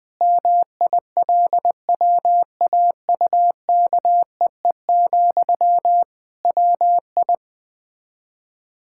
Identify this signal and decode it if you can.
Morse code, heard as a keyed tone, about 20 wpm, 700 Hz. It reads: MILWAUKEE, WI
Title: Practice QSO - Exchange 1 - Beginning - 20wpm